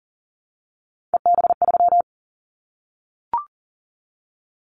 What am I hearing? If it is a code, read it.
Morse code, heard as a keyed tone, about 40 wpm, 700 Hz. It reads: EB3
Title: Callsign Prefixes for Top 25 DXCC - Repeated 3x - 40wpm